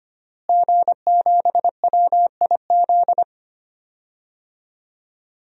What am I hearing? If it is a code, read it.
Morse code, heard as a keyed tone, about 25 wpm, 700 Hz. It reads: G7WIZ
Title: DXCC-100 Call Signs - 5 Characters - Repeated 3x - 25wpm